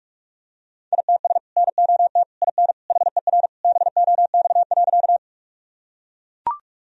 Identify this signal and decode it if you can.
Morse code, heard as a keyed tone, about 41 wpm, 700 Hz. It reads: ITS NOT IN HER BOX.
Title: Sentences from Top 1000 Words - Review Part 2 - 45wpm